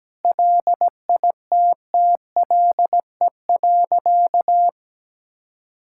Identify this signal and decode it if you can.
Morse code, heard as a keyed tone, about 17 wpm, 700 Hz. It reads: LITTLE.
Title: Sentences from Top 1000 Words - Review Part 5 - 17wpm